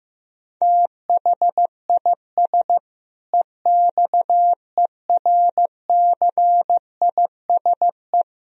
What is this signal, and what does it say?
Morse code, heard as a keyed tone, about 16 wpm, 700 Hz. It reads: THIS EXERCISE
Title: Sentences from Top 1000 Words - Review Part 4 - 15wpm